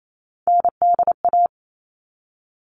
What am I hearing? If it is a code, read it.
Morse code, heard as a keyed tone, about 28 wpm, 700 Hz. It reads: NDA
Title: All Cut Numbers - 3-Digit 28wpm